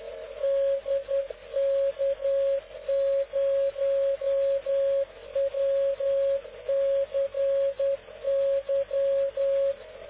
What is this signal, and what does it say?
Morse code, heard as a keyed tone, about 10 wpm, 550 Hz. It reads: DK0WCY
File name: cw / ubacw dk0wcy slow